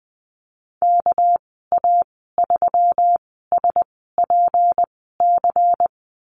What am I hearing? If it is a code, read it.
Morse code, heard as a keyed tone, about 19 wpm, spaced as for 14 wpm, 700 Hz. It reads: KA3SPC
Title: 238 Lesson 38g - Letter q - Call Signs 20wpm